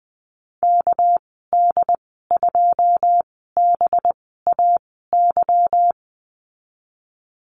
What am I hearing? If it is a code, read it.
Morse code, heard as a keyed tone, about 20 wpm, spaced as for 15 wpm, 700 Hz. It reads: KD2BAY
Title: 231 Lesson 37g - Letter x - Call Signs 20wpm